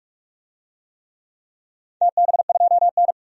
Morse code, heard as a keyed tone, about 46 wpm, 700 Hz. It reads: TB2N
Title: DXCC-100 Call Signs - 4 Characters - 45wpm